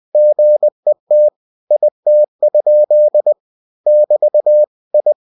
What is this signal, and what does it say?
Morse code, heard as a keyed tone, about 20 wpm, 600 Hz. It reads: GET IT? <BT> I
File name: cw-books / winnie-the-pooh / winnie-20wpm-page-0032